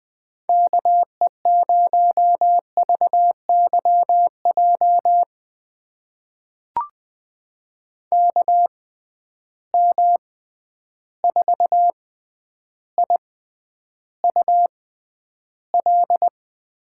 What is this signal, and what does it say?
Morse code, heard as a keyed tone, about 21 wpm, spaced as for 10 wpm, 700 Hz. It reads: KE0VYJ KM4IUL
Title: USA Call Signs - 6 Characters - 6x Character Spacing - 20wpm